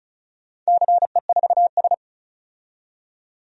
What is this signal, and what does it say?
Morse code, heard as a keyed tone, about 36 wpm, 700 Hz. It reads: CE4S
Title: DXCC-100 Call Signs - 4 Characters - 35wpm